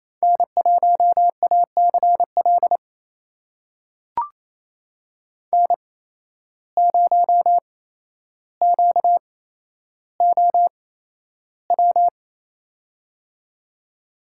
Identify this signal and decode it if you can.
Morse code, heard as a keyed tone, about 28 wpm, spaced as for 12 wpm, 700 Hz. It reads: N1ACL N0QOW
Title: USA Call Signs - 5 Characters - 8x Character Spacing - 28wpm